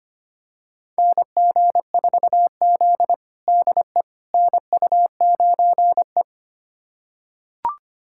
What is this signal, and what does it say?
Morse code, heard as a keyed tone, about 25 wpm, 700 Hz. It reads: NG4Z DE NU9E
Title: USA Callsign DE Callsign - 4 Characters - 25wpm